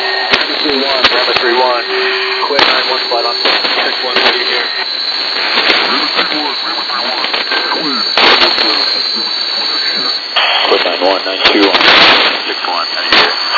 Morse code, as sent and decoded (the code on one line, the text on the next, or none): none